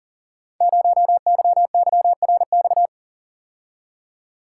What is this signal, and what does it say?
Morse code, heard as a keyed tone, about 40 wpm, 690 Hz. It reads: 0YYRX
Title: Code Groups - 5 Letter:Numbers - 40wpm